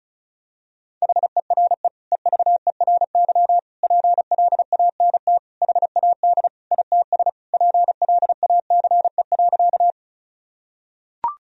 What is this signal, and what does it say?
Morse code, heard as a keyed tone, about 34 wpm, 700 Hz. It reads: HERE EVERY PLANT HAD ITS PLACE.
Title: Sentences from Top 1000 Words - Review Part 1 - 35wpm